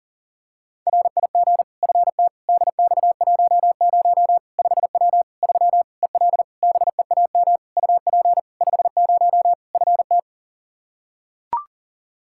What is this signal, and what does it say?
Morse code, heard as a keyed tone, about 40 wpm, 700 Hz. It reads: RIG FT DX10 5W 3 EL BEAM UP 50 FT
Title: Practice QSO - Exchange 2 - Info - 40wpm